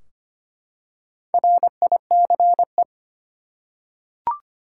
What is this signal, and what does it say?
Morse code, heard as a keyed tone, about 25 wpm, 700 Hz. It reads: RICE
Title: Four Letter Words - 25wpm